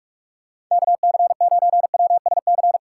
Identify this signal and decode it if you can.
Morse code, heard as a keyed tone, about 44 wpm, 700 Hz. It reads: KC9WSC